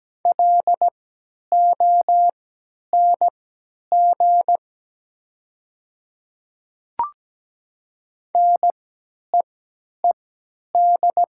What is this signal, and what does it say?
Morse code, heard as a keyed tone, about 17 wpm, spaced as for 10 wpm, 700 Hz. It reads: LONG NEED
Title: 3x Character Spacing - Top 300 Words - 17wpm